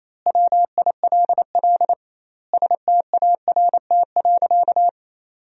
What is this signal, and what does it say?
Morse code, heard as a keyed tone, about 28 wpm, 700 Hz. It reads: WILL START.
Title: 2x Word-Spacing - Sentences from Top 300 Words - 28wpm